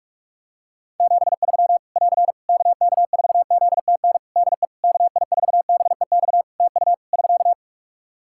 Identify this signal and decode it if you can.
Morse code, heard as a keyed tone, about 46 wpm, 700 Hz. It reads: 73 <AR> KK4ZTN DE KI4BEX TU <SK>
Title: K1USN SST QSO Practice 45wpm